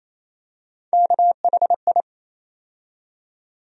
Morse code, heard as a keyed tone, about 28 wpm, 700 Hz. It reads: KHI
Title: USA Call Sign Postfixes - 3 Letters 28wpm